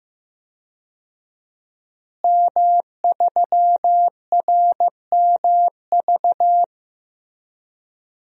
Morse code, heard as a keyed tone, about 15 wpm, 700 Hz. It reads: M3RMV
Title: DXCC-100 Call Signs - 5 Characters - Repeated 2x - 15wpm